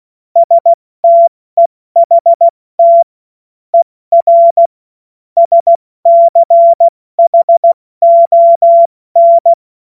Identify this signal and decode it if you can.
Morse code, heard as a keyed tone, about 16 wpm, spaced as for 14 wpm, 680 Hz. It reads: STEHT ER SCHON
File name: Morsekurs / Witze / Witz 05 März 2024 70 F80